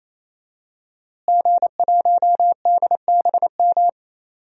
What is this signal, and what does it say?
Morse code, heard as a keyed tone, about 28 wpm, 700 Hz. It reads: G1DBM
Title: DXCC-100 Call Signs - 5 Characters - Repeated 3x - 28wpm